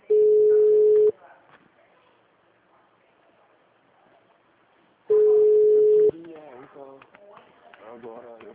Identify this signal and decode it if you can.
Ringback tone (Europe, 425 Hz), with speech.
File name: out- -2013- - - .5039